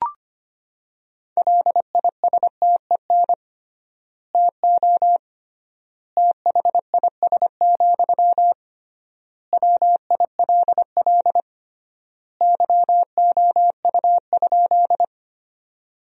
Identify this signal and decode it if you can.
Morse code, heard as a keyed tone, about 25 wpm, 700 Hz. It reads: LISTEN TO THIS, WILL YOU?
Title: 3x Word-Spacing - Sentences from Top 400 Words - 25wpm